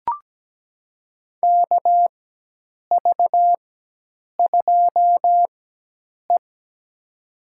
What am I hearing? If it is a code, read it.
Morse code, heard as a keyed tone, about 17 wpm, spaced as for 8 wpm, 700 Hz. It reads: KV2E
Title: USA Call Signs - 4 Characters - 4x Character Spacing - 17wpm